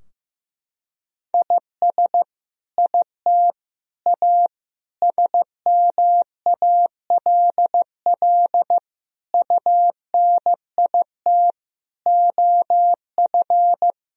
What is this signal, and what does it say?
Morse code, heard as a keyed tone, about 15 wpm, 700 Hz. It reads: IS IT A SMALL UNIT OF
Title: Sentences from Top 1000 Words - Review Part 2 - 15wpm